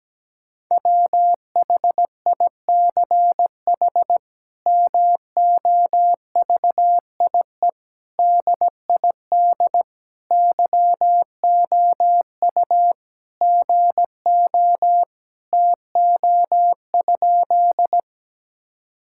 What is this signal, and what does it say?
Morse code, heard as a keyed tone, about 17 wpm, 700 Hz. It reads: WHICH MOVIE DID YOU GO TO?
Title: Sentences from Top 1000 Words - Review Part 5 - 17wpm